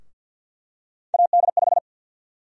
Morse code, heard as a keyed tone, about 52 wpm, 700 Hz. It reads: AD5
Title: USA Call Sign Prefixes - Repeated 2x - 50wpm